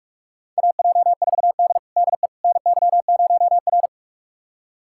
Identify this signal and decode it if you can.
Morse code, heard as a keyed tone, about 46 wpm, 700 Hz. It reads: AJ4D DE NY0R
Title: USA Callsign DE Callsign - 4 Characters - 45wpm